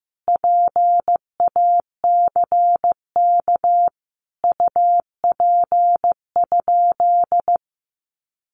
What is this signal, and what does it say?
Morse code, heard as a keyed tone, about 15 wpm, 700 Hz. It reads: PACK UP?
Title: Sentences from Top 1300 Words 15wpm